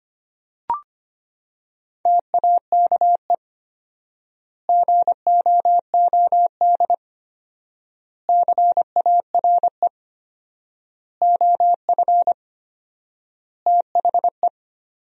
Morse code, heard as a keyed tone, about 25 wpm, 700 Hz. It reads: TAKE GOOD CARE OF THE
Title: 4x Word-Spacing - Sentences from Top 400 Words - 25wpm